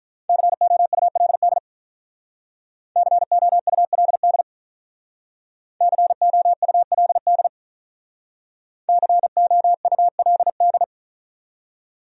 Morse code, heard as a keyed tone, about 45 wpm, 700 Hz. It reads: COULD COULD COULD COULD